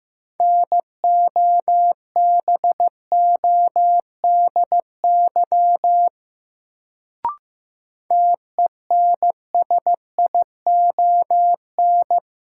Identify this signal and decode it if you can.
Morse code, heard as a keyed tone, about 15 wpm, 700 Hz. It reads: NOBODY TENSION
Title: Sentences from Top 1600 Words - New Word Drill - 15wpm